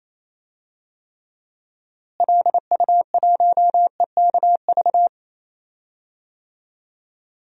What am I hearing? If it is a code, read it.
Morse code, heard as a keyed tone, about 28 wpm, 700 Hz. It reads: LU1EKV
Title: DXCC-100 Call Signs - 6 Characters - Repeated 3x - 28wpm